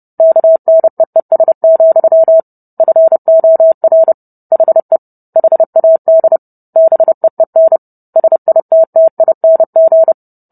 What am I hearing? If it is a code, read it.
Morse code, heard as a keyed tone, about 30 wpm, 650 Hz. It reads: KNEES, FOR HE HAD BEEN SITTING